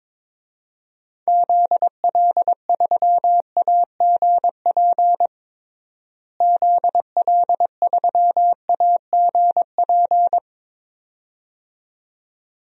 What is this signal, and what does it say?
Morse code, heard as a keyed tone, about 23 wpm, 700 Hz. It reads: ZL3AGP ZL3AGP
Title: DXCC-100 Call Signs - 6 Characters - Repeated 2x - 22wpm